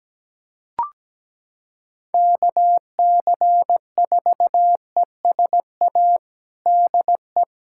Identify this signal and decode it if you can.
Morse code, heard as a keyed tone, about 17 wpm, 700 Hz. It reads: KC4ESA DE
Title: USA Callsign DE Callsign - 6 Characters - 17wpm